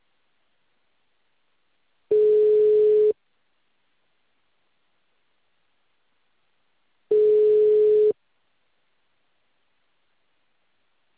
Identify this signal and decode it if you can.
Ringback tone (Europe, 425 Hz).